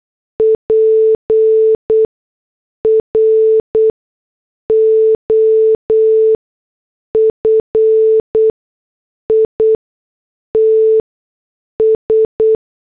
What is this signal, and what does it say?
Morse code, heard as a keyed tone, about 8 wpm, spaced as for 6 wpm, 440 Hz. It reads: PROFITS
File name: morse / headlines